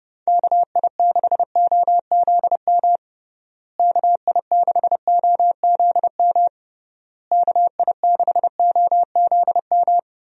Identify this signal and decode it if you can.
Morse code, heard as a keyed tone, about 29 wpm, 700 Hz. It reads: KI6OZM KI6OZM KI6OZM
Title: USA Call Signs - 6 Characters - Repeated 3x - 30wpm